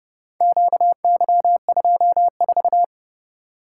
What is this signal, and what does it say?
Morse code, heard as a keyed tone, about 30 wpm, 700 Hz. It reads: QY24